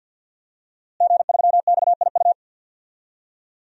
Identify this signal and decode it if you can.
Morse code, heard as a keyed tone, about 51 wpm, 700 Hz. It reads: G3XIU
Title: DXCC-100 Call Signs - 5 Characters - Repeated 2x - 50wpm